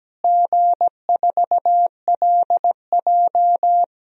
Morse code, heard as a keyed tone, about 17 wpm, 700 Hz. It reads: G4LJ
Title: DXCC-100 Call Signs - 4 Characters - Repeated 3x - 17wpm